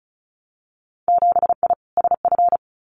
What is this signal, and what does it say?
Morse code, heard as a keyed tone, about 34 wpm, 700 Hz. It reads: 7I SF
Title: Winter Field Day Exchanges Encore 35wpm